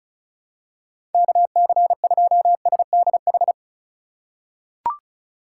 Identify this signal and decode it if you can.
Morse code, heard as a keyed tone, about 36 wpm, 700 Hz. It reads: KC2SDH
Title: USA Call Signs - 6 Characters - 2x Character Spacing - 35wpm